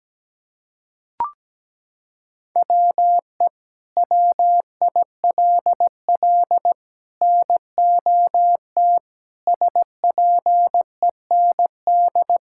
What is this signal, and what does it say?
Morse code, heard as a keyed tone, about 17 wpm, 700 Hz. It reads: WE WILL NOT SPEND
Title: Sentences from Top 1000 Words - Review Part 5 - 17wpm